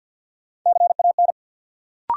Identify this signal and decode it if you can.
Morse code, heard as a keyed tone, about 53 wpm, 700 Hz. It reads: CAN
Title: Sentences from Top 100 Words - New Word Drill - 50wpm